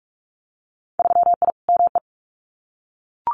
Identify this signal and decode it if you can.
Morse code, heard as a keyed tone, about 46 wpm, 700 Hz. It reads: 3I ME
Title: Winter Field Day Exchanges 45wpm